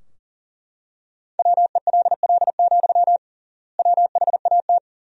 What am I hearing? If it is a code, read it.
Morse code, heard as a keyed tone, about 37 wpm, 700 Hz. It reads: WELL, WHAT
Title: 3x Word-Spacing - Sentences from Top 300 Words - 40wpm